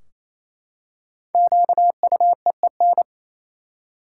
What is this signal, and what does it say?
Morse code, heard as a keyed tone, about 28 wpm, 700 Hz. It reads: QUEEN
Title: Five Letter Words - 28wpm